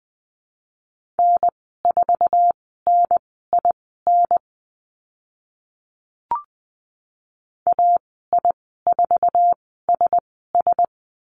Morse code, heard as a keyed tone, about 20 wpm, spaced as for 14 wpm, 700 Hz. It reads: N4NIN AI4SS
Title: 069 Lesson 12g - Letter d - Call Signs 20wpm